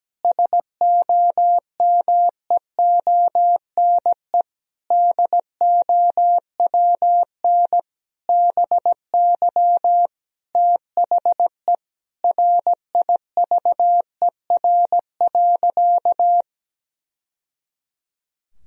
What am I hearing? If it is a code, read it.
Morse code, heard as a keyed tone, about 17 wpm, 700 Hz. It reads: SOMEONE DOWN BY THE RIVER.
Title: Sentences from Top 1000 Words - Review Part 4 - 17wpm